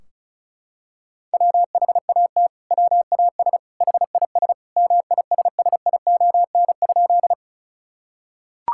Morse code, heard as a keyed tone, about 36 wpm, 700 Hz. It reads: WHAT WAS HIS MISSION?
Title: Sentences from Top 1600 Words 35wpm